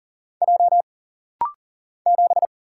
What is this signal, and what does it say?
Morse code, heard as a keyed tone, about 40 wpm, 700 Hz. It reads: J 7
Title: Single Letter-Number - Rapid-Fire 40wpm